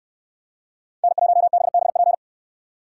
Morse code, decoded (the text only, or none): N1BBP